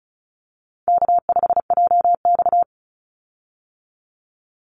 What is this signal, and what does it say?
Morse code, heard as a keyed tone, about 34 wpm, 700 Hz. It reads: K5JX
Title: Call Signs Repeated 2x - 35wpm